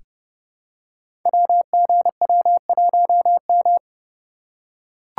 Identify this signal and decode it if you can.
Morse code, heard as a keyed tone, about 30 wpm, 700 Hz. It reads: WGW1M
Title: Code Groups - 5 Letter:Numbers - 30wpm